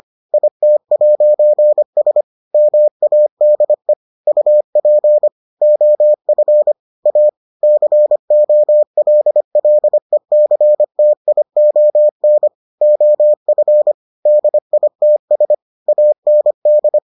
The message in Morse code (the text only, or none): IT'S MADE UP OF A COLLECTION OF DITS AND